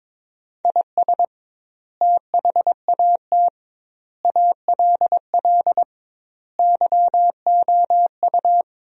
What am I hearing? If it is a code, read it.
Morse code, heard as a keyed tone, about 22 wpm, 700 Hz. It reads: IS THAT ALL YOU
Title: 2x Word-Spacing - Sentences from Top 200 Words - 22wpm